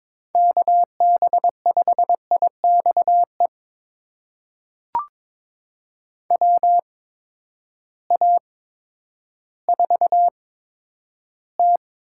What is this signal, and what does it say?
Morse code, heard as a keyed tone, about 22 wpm, spaced as for 11 wpm, 700 Hz. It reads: KB5IXE WA4T
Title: USA Call Signs - 6 Characters - 8x Character Spacing - 22wpm